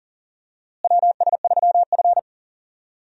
Morse code, heard as a keyed tone, about 38 wpm, 700 Hz. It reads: WS3F
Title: USA Call Signs - 4 Characters - Repeated 2x - 40wpm